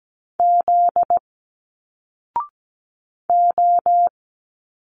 Morse code, heard as a keyed tone, about 17 wpm, 700 Hz. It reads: Z O
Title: Single Letters 17wpm